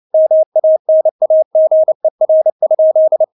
Morse code, heard as a keyed tone, about 29 wpm, 630 Hz. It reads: MANAGER?
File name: Witz10 29wpm